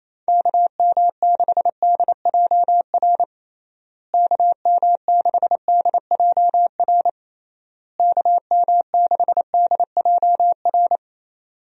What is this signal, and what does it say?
Morse code, heard as a keyed tone, about 27 wpm, 700 Hz. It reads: KM6DJR KM6DJR KM6DJR
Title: USA Call Signs - 6 Characters - Repeated 3x - 28wpm